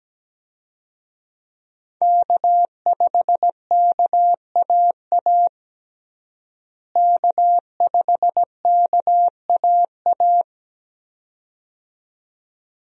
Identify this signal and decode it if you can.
Morse code, heard as a keyed tone, about 17 wpm, 700 Hz. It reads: K5KAA K5KAA
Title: USA Call Signs - 5 Characters - Repeated 2x - 17wpm